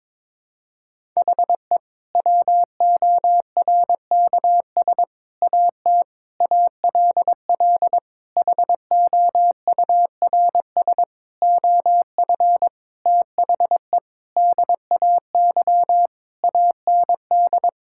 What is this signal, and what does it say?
Morse code, heard as a keyed tone, about 22 wpm, 700 Hz. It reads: HE WORKS AT ALL HOURS OF THE DAY AND
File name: practice / 22wpm / Sentences with 8-Word Phrases - 22wpm